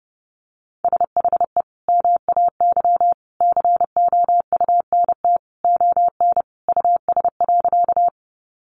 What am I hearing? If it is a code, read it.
Morse code, heard as a keyed tone, about 30 wpm, 700 Hz. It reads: SHE MAY COUNT ON US.
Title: Sentences from Top 1500 Words 30wpm